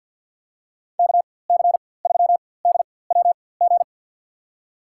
Morse code, heard as a keyed tone, about 52 wpm, spaced as for 24 wpm, 700 Hz. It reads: KC3DWG
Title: USA Call Signs - 6 Characters - 4x Character Spacing - 50wpm